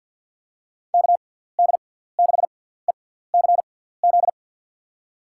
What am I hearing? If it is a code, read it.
Morse code, heard as a keyed tone, about 53 wpm, spaced as for 17 wpm, 700 Hz. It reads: KD6ECZ
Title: USA Call Signs - 6x Character Spacing - 50wpm